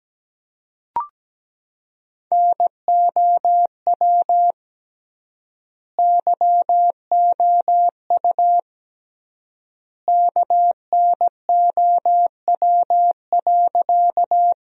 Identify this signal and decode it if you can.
Morse code, heard as a keyed tone, about 17 wpm, 700 Hz. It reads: NOW YOU KNOW.
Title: 3x Word-Spacing - Sentences from Top 100 Words - 17wpm